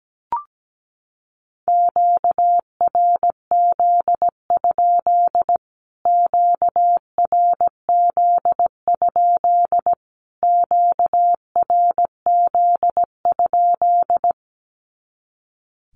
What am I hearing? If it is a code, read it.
Morse code, heard as a keyed tone, about 17 wpm, 700 Hz. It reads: QRZ? QRZ? QRZ?
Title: Q-Codes Repeated 3x 17wpm